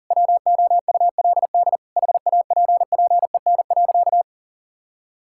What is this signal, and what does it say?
Morse code, heard as a keyed tone, about 40 wpm, 700 Hz. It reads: WOULD HAPPEN.
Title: Sentences from Top 1000 Words - Review Part 6 - 40wpm